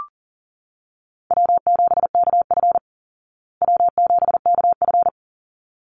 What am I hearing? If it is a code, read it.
Morse code, heard as a keyed tone, about 40 wpm, 700 Hz. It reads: W7KF W7KF
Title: Easy Call Signs Repeated 2x - 40wpm